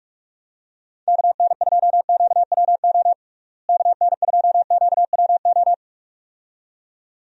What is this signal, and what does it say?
Morse code, heard as a keyed tone, about 46 wpm, 700 Hz. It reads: KN2QWO KN2QWO
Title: USA Call Signs - 6 Characters - Repeated 2x - 45wpm